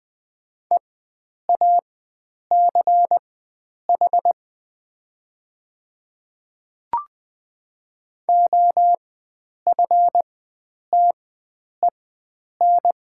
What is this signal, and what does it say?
Morse code, heard as a keyed tone, about 20 wpm, spaced as for 9 wpm, 700 Hz. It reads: EACH OFTEN
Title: 4x Character Spacing - Top 300 Words - 20wpm